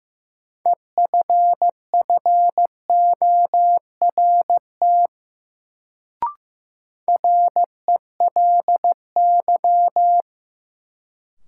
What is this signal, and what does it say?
Morse code, heard as a keyed tone, about 15 wpm, 700 Hz. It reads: EFFORT RELY
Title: Sentences from Top 1200 Words - New Word Drill Alt - 15wpm